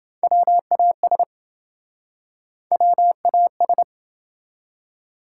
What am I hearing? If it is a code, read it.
Morse code, heard as a keyed tone, about 28 wpm, 700 Hz. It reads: WAS WAS